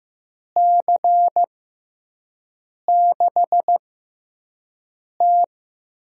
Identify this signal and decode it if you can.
Morse code, heard as a keyed tone, about 15 wpm, spaced as for 5 wpm, 700 Hz. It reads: C6T
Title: USA Call Signs - 6 Characters - 6x Character Spacing - 15wpm